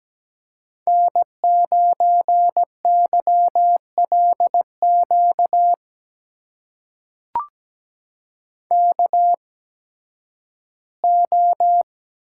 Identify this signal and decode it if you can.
Morse code, heard as a keyed tone, about 17 wpm, spaced as for 11 wpm, 700 Hz. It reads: N9YLQ KO
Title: USA Call Signs - 5 Characters - 8x Character Spacing - 17wpm